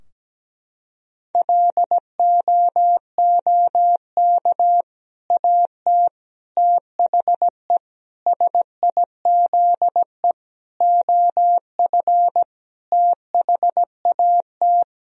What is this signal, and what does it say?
Morse code, heard as a keyed tone, about 17 wpm, 700 Hz. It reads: LOOK AT THE SIZE OF THAT
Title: Sentences from Top 1000 Words - Review Part 3 - 17wpm